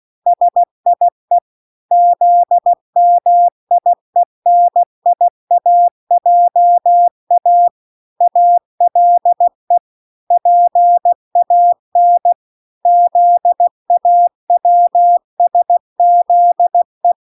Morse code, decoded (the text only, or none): SIE ZMIENIAJA ALE PAN ZAWSZE